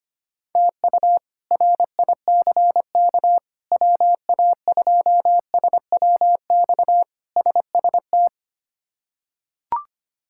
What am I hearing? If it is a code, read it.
Morse code, heard as a keyed tone, about 25 wpm, 700 Hz. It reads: TU RICK WA2SWX SST
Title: K1USN SST QSO Practice - 25wpm